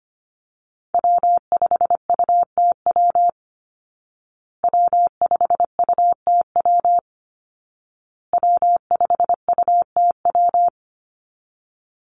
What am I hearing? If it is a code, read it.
Morse code, heard as a keyed tone, about 25 wpm, 700 Hz. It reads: W5UTW W5UTW W5UTW
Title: Call Signs Repeated 2x - 25wpm